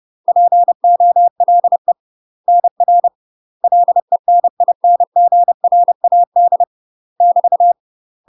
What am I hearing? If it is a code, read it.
Morse code, heard as a keyed tone, about 30 wpm, 700 Hz. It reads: POLE NR LENINGRAD <BT>
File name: UK MASTER QSOs TRAINING / 30WPM / UK MASTER QSOs 0012